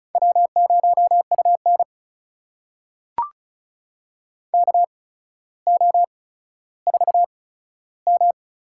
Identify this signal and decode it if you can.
Morse code, heard as a keyed tone, about 36 wpm, spaced as for 15 wpm, 700 Hz. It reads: W0UN KO4M
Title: USA Call Signs - 4 Characters - 8x Character Spacing - 35wpm